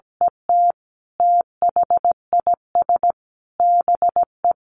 Morse code, heard as a keyed tone, about 17 wpm, 700 Hz. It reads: ET THIS BE
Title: Sentences from Top 1400 Words 17wpm